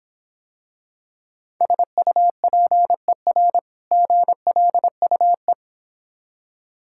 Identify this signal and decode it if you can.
Morse code, heard as a keyed tone, about 26 wpm, 700 Hz. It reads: SUPER GLUE